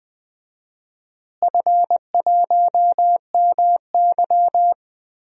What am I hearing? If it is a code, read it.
Morse code, heard as a keyed tone, about 20 wpm, 700 Hz. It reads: F1MY
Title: DXCC-100 Call Signs - 4 Characters - Repeated 2x - 20wpm